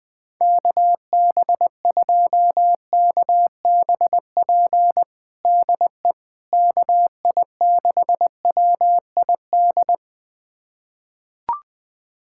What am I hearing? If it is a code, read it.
Morse code, heard as a keyed tone, about 20 wpm, 700 Hz. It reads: KB2KBP DE KI6WID
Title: USA Callsign DE Callsign - 6 Characters - 20wpm